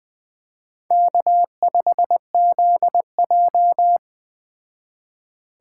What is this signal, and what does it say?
Morse code, heard as a keyed tone, about 21 wpm, 700 Hz. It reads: K5ZJ